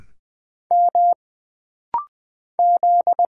Morse code, heard as a keyed tone, about 20 wpm, 700 Hz. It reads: M Z
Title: Single Letters - Rapid-Fire 20wpm